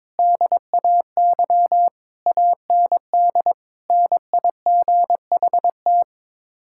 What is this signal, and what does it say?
Morse code, heard as a keyed tone, about 21 wpm, 700 Hz. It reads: DAY AND NIGHT
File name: practice / 22wpm / Sets of 8 Words - 22wpm